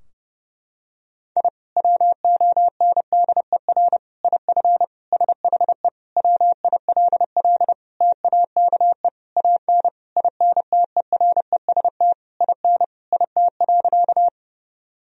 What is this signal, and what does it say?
Morse code, heard as a keyed tone, about 32 wpm, 700 Hz. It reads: I WONDER IF SHE WILL TAKE AN INTEREST IN IT.
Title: Sentences from Top 1000 Words - Review Part 3 - 30wpm